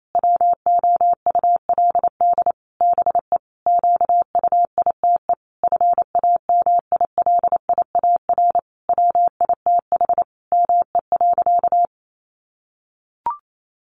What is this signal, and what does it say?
Morse code, heard as a keyed tone, about 28 wpm, 700 Hz. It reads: WOULD BE QUITE FAMILIAR WITH ME.
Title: Sentences from Top 1000 Words 28wpm